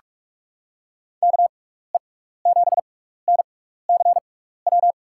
Morse code, heard as a keyed tone, about 44 wpm, spaced as for 15 wpm, 700 Hz. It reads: KE7NCW
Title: USA Call Signs - 6 Characters - 6x Character Spacing - 45wpm